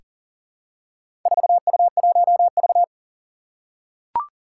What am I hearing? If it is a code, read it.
Morse code, heard as a keyed tone, about 42 wpm, 700 Hz. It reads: 4U1V
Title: Four Challenging Letter-Number Combinations 40wpm